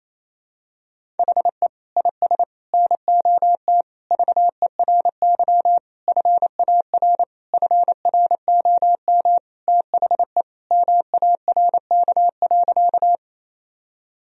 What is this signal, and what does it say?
Morse code, heard as a keyed tone, about 28 wpm, 700 Hz. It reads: HE IS NOT VERY FAR FROM THE MARK.
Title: Sentences from Top 1000 Words - Review Part 1 - 28wpm